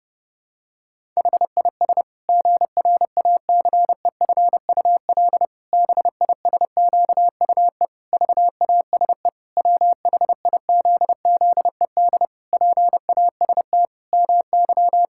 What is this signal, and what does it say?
Morse code, heard as a keyed tone, about 30 wpm, 700 Hz. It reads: HIS GRACEFUL BISQUE VASE WHIZZED PAST MY
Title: Pangram Challenge v2 - 30wpm